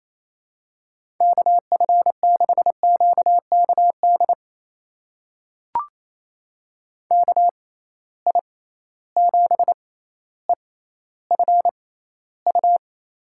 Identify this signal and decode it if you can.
Morse code, heard as a keyed tone, about 28 wpm, spaced as for 14 wpm, 700 Hz. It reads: KF6QKD KI7EFU
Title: USA Call Signs - 6 Characters - 6x Character Spacing - 28wpm